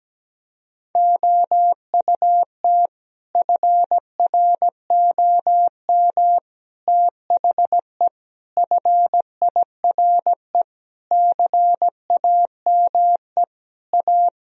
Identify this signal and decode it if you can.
Morse code, heard as a keyed tone, about 17 wpm, 700 Hz. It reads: OUT FROM THE FIRE CAME A
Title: Sentences from Top 1000 Words - Review Part 3 - 17wpm